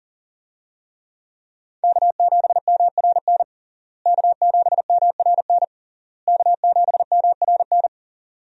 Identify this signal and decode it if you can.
Morse code, heard as a keyed tone, about 37 wpm, 700 Hz. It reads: K7MRN K7MRN K7MRN
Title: USA Call Signs - 5 Characters - Repeated 3x - 40wpm